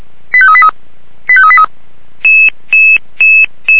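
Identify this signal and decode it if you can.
Five-tone selective call (ZVEI), heard as quick runs of five tones, tones of about 70 ms. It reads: 85383 85383
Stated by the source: ZVEI five-tone selective call received by SDR.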